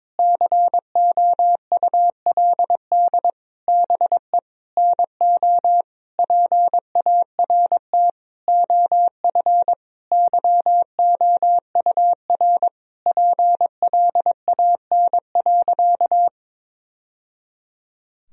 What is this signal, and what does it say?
Morse code, heard as a keyed tone, about 22 wpm, 700 Hz. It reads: COULD BE NO PART OF YOUR PLAN.
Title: Sentences from Top 1000 Words - Review Part 2 - 22wpm